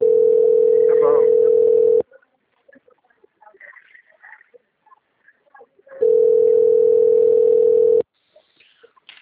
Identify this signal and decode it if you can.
Ringback tone (North America), 440 + 480 Hz.
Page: Capital - SP